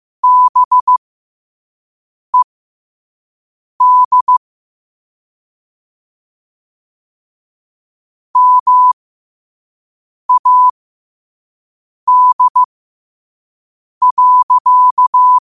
Morse code, heard as a keyed tone, about 15 wpm, spaced as for 5 wpm, 1000 Hz. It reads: BED MAD.
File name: qotdfiles / 5wpm